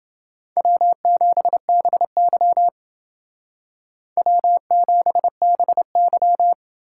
Morse code, heard as a keyed tone, about 28 wpm, 690 Hz. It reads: W7BY W7BY